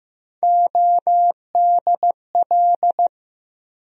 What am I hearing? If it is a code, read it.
Morse code, heard as a keyed tone, about 15 wpm, 700 Hz. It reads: ODL
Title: USA Call Sign Postfixes - 3 Letters 15wpm